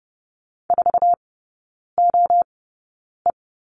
Morse code, heard as a keyed tone, about 30 wpm, spaced as for 9 wpm, 690 Hz. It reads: 4OE
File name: Lesson 17e - Letter u - Three Characters 30wpm